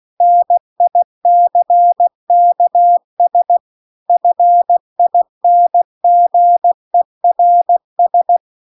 Morse code, heard as a keyed tone, about 16 wpm, 700 Hz. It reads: NICKS FINGERS
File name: CW-The-Big-Twoheaded-River-2-16WPM